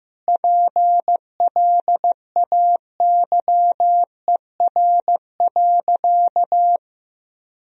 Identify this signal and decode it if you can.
Morse code, heard as a keyed tone, about 15 wpm, 700 Hz. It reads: PLAYER.
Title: Sentences from Top 1000 Words - Review Part 4 - 15wpm